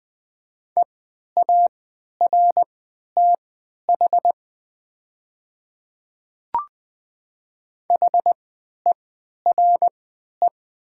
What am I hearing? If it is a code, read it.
Morse code, heard as a keyed tone, about 20 wpm, spaced as for 11 wpm, 690 Hz. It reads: EARTH HERE
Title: 3x Character Spacing - Top 200 Words - 20wpm